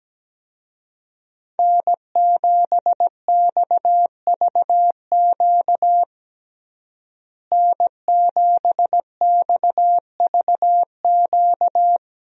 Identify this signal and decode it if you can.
Morse code, heard as a keyed tone, about 17 wpm, 700 Hz. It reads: N7XVQ N7XVQ
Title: USA Call Signs - 5 Characters - Repeated 2x - 17wpm